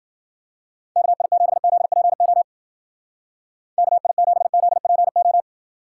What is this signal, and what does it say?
Morse code, heard as a keyed tone, about 48 wpm, 700 Hz. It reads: KI7ZPO KI7ZPO
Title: Speed Racing - USA Call Signs 40wpm